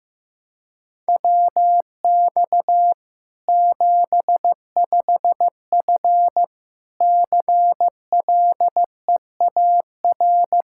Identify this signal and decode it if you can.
Morse code, heard as a keyed tone, about 15 wpm, 700 Hz. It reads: WX 75F CLEAR
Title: Practice QSO - Exchange 2 - Info - 15wpm